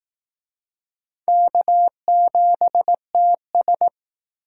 Morse code, heard as a keyed tone, about 19 wpm, 700 Hz. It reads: K7TS
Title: Speed Racing - USA 1x2 Call Signs 15wpm